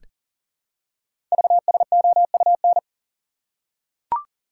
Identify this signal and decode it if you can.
Morse code, heard as a keyed tone, about 41 wpm, 700 Hz. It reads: VSOUN
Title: Code Groups - 5 Letters - 40wpm